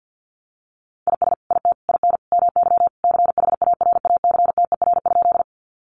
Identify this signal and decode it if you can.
Morse code, heard as a keyed tone, about 50 wpm, 700 Hz. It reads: IS IT IN MY CHARACTER?
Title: Sentences from Top 1200 Words 50wpm